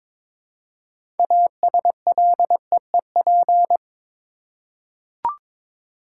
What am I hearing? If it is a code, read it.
Morse code, heard as a keyed tone, about 22 wpm, 700 Hz. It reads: ASLEEP
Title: Six Letter Words - 22wpm